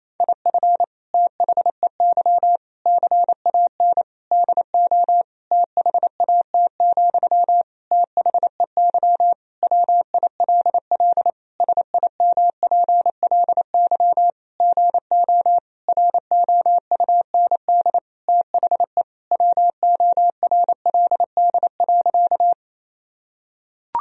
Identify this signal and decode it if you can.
Morse code, heard as a keyed tone, about 28 wpm, 700 Hz. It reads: IF THEY CAN DO THAT, THEY WILL SIMPLY GO ROUND THE WORLD.
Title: Sentences from Top 1000 Words - Review Part 4 - 28wpm